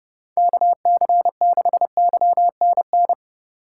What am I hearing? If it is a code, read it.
Morse code, heard as a keyed tone, about 31 wpm, 700 Hz. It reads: KC6YNN
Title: USA Call Signs - 6 Characters - 3x Character Spacing - 30wpm